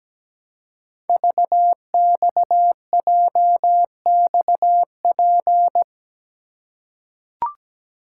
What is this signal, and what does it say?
Morse code, heard as a keyed tone, about 17 wpm, 700 Hz. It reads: VXJXP
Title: Code Groups - 5 Letter:Numbers - 17wpm